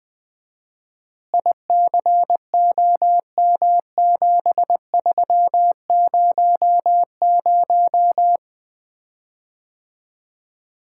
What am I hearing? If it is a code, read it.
Morse code, heard as a keyed tone, about 21 wpm, 700 Hz. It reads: ICOM7300
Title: Common QSO Phrases - 4x Word-Spacing - 20wpm